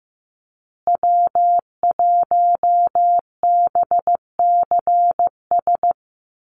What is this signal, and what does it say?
Morse code, heard as a keyed tone, about 15 wpm, 700 Hz. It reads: W1BCS
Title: USA Call Signs Repeated 3x 15wpm